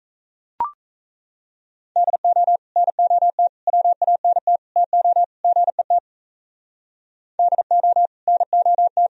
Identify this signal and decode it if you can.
Morse code, heard as a keyed tone, about 36 wpm, 700 Hz. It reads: DO NOT WANT TO GET DO NOT